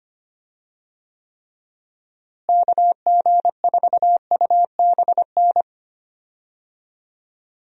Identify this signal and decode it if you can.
Morse code, heard as a keyed tone, about 25 wpm, 700 Hz. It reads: KG4UBN
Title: USA Call Signs - 6 Characters - Repeated 3x - 25wpm